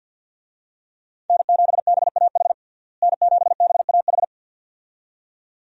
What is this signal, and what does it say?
Morse code, heard as a keyed tone, about 47 wpm, 700 Hz. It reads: N7BAH N7BAH
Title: USA Call Signs - 5 Characters - Repeated 2x - 50wpm